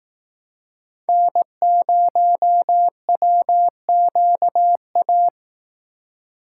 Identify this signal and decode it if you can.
Morse code, heard as a keyed tone, about 19 wpm, 700 Hz. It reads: N0WQA
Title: Speed Racing - USA 1x3 Call Signs 15wpm